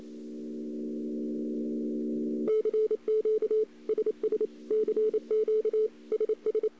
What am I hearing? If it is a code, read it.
Morse code, heard as a keyed tone, about 28 wpm, 440 Hz. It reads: CQ SS CQ SS
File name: CW / 2008-ss-cw-k5tr-op-n5rz -02:00